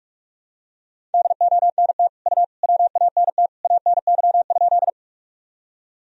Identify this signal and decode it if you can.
Morse code, heard as a keyed tone, about 46 wpm, 700 Hz. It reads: DONT U WANT ANY?
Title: Sentences from Top 300 Words and Abbreviations 45wpm